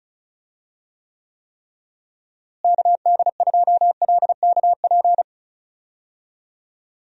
Morse code, heard as a keyed tone, about 34 wpm, 700 Hz. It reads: KD2LKP
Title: USA Call Signs - 6 Characters - Repeated 3x - 35wpm